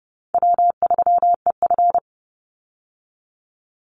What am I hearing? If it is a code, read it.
Morse code, heard as a keyed tone, about 30 wpm, 700 Hz. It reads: W3EF
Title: Easy Call Signs Repeated 3x - 30wpm